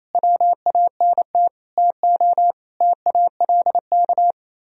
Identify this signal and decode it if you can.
Morse code, heard as a keyed tone, about 36 wpm, 700 Hz. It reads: WANT TO TALK